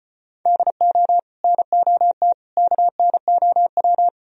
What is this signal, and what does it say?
Morse code, heard as a keyed tone, about 38 wpm, 700 Hz. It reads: DO NOT KNOW